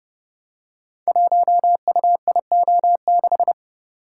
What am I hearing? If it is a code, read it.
Morse code, heard as a keyed tone, about 30 wpm, 700 Hz. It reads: 1UIO6
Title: Code Groups - 5 Letter:Numbers - 30wpm